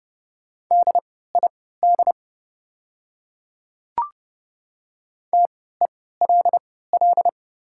Morse code, heard as a keyed tone, about 30 wpm, spaced as for 17 wpm, 690 Hz. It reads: DID TELL
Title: 3x Character Spacing - Top 200 Words - 30wpm